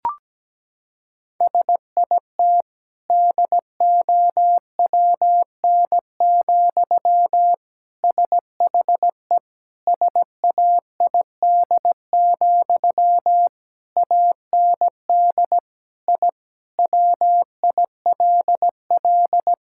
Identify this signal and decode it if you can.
Morse code, heard as a keyed tone, about 17 wpm, 700 Hz. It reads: SIT DOWN, SHE SAID, AND I WILL
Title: Sentences from Top 1000 Words - Review Part 6 - 17wpm